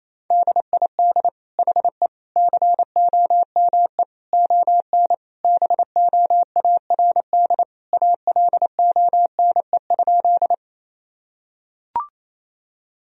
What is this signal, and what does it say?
Morse code, heard as a keyed tone, about 28 wpm, 700 Hz. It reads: DID HE COME ON BOARD ALONE?
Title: Sentences from Top 1000 Words - Review Part 4 - 28wpm